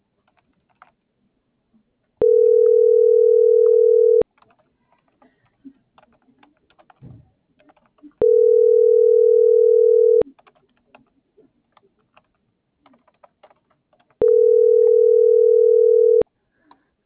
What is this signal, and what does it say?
Ringback tone (North America), 440 + 480 Hz.